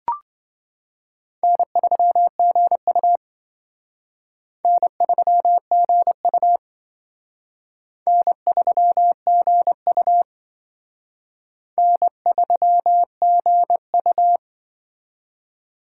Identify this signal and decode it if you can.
Morse code, heard as a keyed tone, about 25 wpm, 700 Hz. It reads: N3GU N3GU N3GU N3GU
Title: Speed Racing - USA 1x2 Call Signs 20wpm